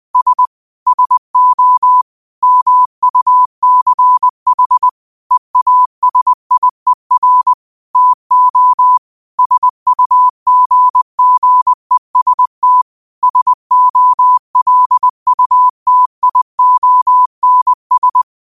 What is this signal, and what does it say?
Morse code, heard as a keyed tone, about 20 wpm, 1000 Hz. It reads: S SO MUCH EASIER TO SUGGEST SOLUTIONS
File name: qotdfiles / 20wpm